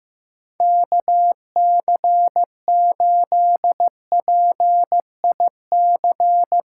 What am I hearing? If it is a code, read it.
Morse code, heard as a keyed tone, about 14 wpm, 700 Hz. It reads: KC8PIC
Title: USA Callsign DE Callsign - 6 Characters - 15wpm